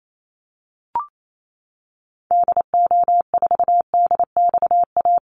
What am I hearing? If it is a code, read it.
Morse code, heard as a keyed tone, about 28 wpm, 700 Hz. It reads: DO4DXA
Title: Call Signs Repeated 3x - 28wpm